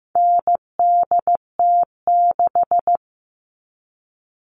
Morse code, heard as a keyed tone, about 15 wpm, 700 Hz. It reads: NDT6
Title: All Cut Numbers - 4-Digit 15wpm